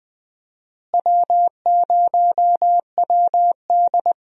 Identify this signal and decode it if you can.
Morse code, heard as a keyed tone, about 20 wpm, 700 Hz. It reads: W0WD
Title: USA Call Signs - 4 Characters - Repeated 2x - 20wpm